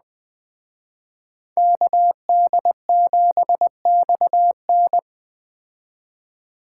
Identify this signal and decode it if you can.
Morse code, heard as a keyed tone, about 21 wpm, 700 Hz. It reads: KD7XN
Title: Speed Racing - USA 2x2 Call Signs 17wpm